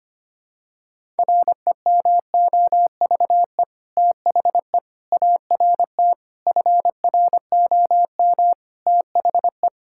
Morse code, heard as a keyed tone, about 25 wpm, 700 Hz. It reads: REMOVE THE ART FROM THE
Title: Sentences from Top 1000 Words - Review Part 6 - 25wpm